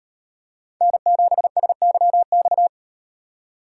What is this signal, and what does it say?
Morse code, heard as a keyed tone, about 36 wpm, 700 Hz. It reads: N7SYX